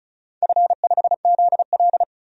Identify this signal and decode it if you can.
Morse code, heard as a keyed tone, about 36 wpm, 700 Hz. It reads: F5ZL
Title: DXCC-100 Call Signs - 4 Characters - 35wpm